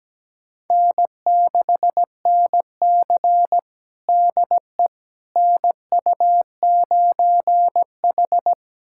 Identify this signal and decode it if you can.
Morse code, heard as a keyed tone, about 17 wpm, 700 Hz. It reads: N6NC DE NU9H
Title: USA Callsign DE Callsign - 4 Characters - 17wpm